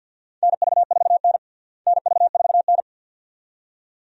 Morse code, heard as a keyed tone, about 52 wpm, 700 Hz. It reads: NV4N NV4N
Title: USA Call Signs - 4 Characters - Repeated 2x - 50wpm